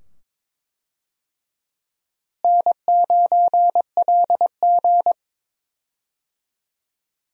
Morse code, heard as a keyed tone, about 22 wpm, 700 Hz. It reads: N9LG
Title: USA Call Signs - 4 Characters - Repeated 2x - 22wpm